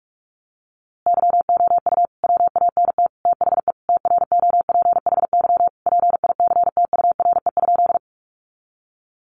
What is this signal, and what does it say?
Morse code, heard as a keyed tone, about 44 wpm, 700 Hz. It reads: YOU WANT THE TROPHY PICTURE?
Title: A Sound Of Thunder 45wpm